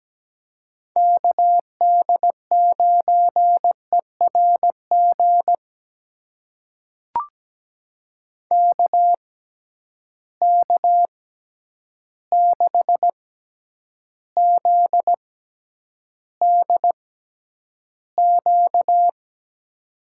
Morse code, heard as a keyed tone, about 17 wpm, spaced as for 9 wpm, 700 Hz. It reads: KD9ERG KK6ZDQ
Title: USA Call Signs - 6 Characters - 6x Character Spacing - 17wpm